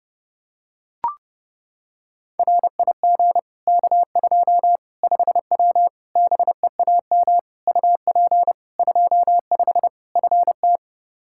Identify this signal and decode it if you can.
Morse code, heard as a keyed tone, about 30 wpm, 700 Hz. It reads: RIG K2 5W BEAM UP 25 FT
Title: Practice QSO - Exchange 2 - Info - 30wpm